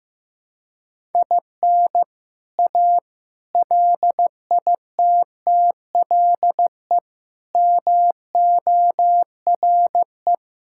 Morse code, heard as a keyed tone, about 15 wpm, 700 Hz. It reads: IN A LITTLE MORE
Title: Sentences from Top 1000 Words - Review Part 5 - 15wpm